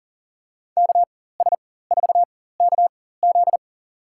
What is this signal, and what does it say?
Morse code, heard as a keyed tone, about 40 wpm, spaced as for 19 wpm, 700 Hz. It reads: KS4KZ
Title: USA Call Signs - 4x Character Spacing - 40wpm